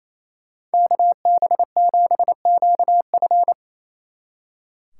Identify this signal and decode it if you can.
Morse code, heard as a keyed tone, about 37 wpm, 700 Hz. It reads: KB7QF